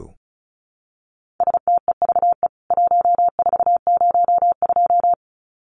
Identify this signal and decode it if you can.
Morse code, heard as a keyed tone, about 34 wpm, 700 Hz. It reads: STEVE 1402
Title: CWT Exchange 35wpm